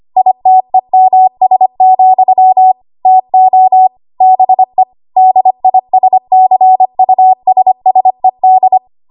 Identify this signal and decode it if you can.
Morse code, heard as a keyed tone, about 25 wpm, 750 Hz. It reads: ITEMS, TO BE DISCUSSED
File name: Archive / 25 WPM / 25WPM